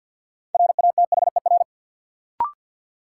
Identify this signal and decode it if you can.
Morse code, heard as a keyed tone, about 51 wpm, 700 Hz. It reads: RATHER
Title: Sets of 2 Words - New Word Drill - 50wpm